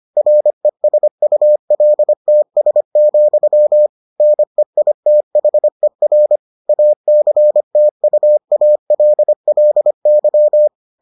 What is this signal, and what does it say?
Morse code, heard as a keyed tone, about 25 wpm, 600 Hz. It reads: RESULTS, NEITHER ACTUALLY